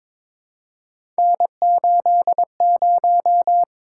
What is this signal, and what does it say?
Morse code, heard as a keyed tone, about 22 wpm, 700 Hz. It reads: N80
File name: practice / 22wpm / Three Letter-Number Combinations - 22wpm